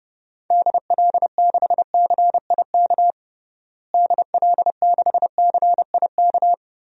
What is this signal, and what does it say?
Morse code, heard as a keyed tone, about 30 wpm, 700 Hz. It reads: DL6CIK DL6CIK
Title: DXCC-100 Call Signs - 6 Characters - Repeated 3x - 30wpm